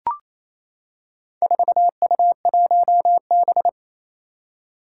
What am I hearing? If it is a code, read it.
Morse code, heard as a keyed tone, about 28 wpm, 700 Hz. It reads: 4U1B
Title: Four Challenging Letter-Number Combinations 28wpm